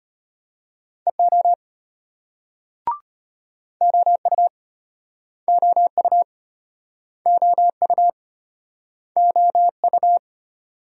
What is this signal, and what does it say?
Morse code, heard as a keyed tone, about 36 wpm, 700 Hz. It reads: EO OU OU OU OU